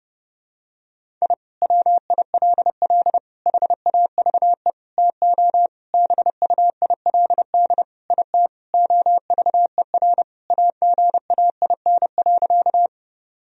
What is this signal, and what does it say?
Morse code, heard as a keyed tone, about 32 wpm, 700 Hz. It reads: I WILL HAVE TO BUILD IT OVER AGAIN.
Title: Sentences from Top 1000 Words - Review Part 1 - 30wpm